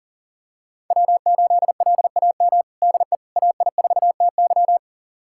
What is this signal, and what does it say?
Morse code, heard as a keyed tone, about 40 wpm, 700 Hz. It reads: W8LAM DE AI4TY
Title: USA Callsign DE Callsign - 5 Characters - 40wpm